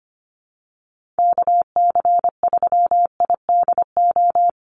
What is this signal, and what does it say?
Morse code, heard as a keyed tone, about 25 wpm, 700 Hz. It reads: KC3IDO
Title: USA Call Signs 25wpm